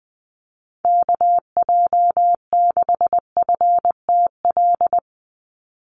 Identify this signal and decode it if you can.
Morse code, heard as a keyed tone, about 20 wpm, 700 Hz. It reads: KJ6FTL
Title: USA Call Signs 20wpm